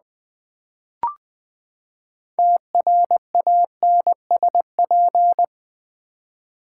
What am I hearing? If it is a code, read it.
Morse code, heard as a keyed tone, about 20 wpm, 700 Hz. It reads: TRANSP
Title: Build A Word - 6 Letters At A Time - 12-Letter Words - 20wpm